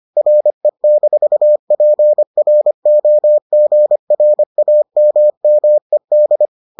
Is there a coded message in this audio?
Yes: Morse code from 0.2 to 6.5 s, heard as a keyed tone, about 25 wpm, 600 Hz. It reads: RE-PROGRAMMED